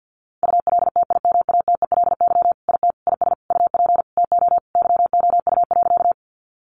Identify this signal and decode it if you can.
Morse code, heard as a keyed tone, about 50 wpm, 700 Hz. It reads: ULTIMATELY IT IS UP TO YOU.
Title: Sentences from Top 1200 Words 50wpm